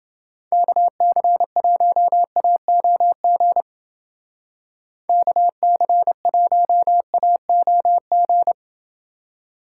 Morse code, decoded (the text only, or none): KC1AOG KC1AOG